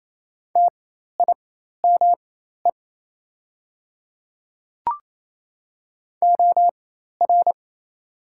Morse code, heard as a keyed tone, about 27 wpm, spaced as for 13 wpm, 700 Hz. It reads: TIME OR
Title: 4x Character Spacing - Top 100 Words - 28wpm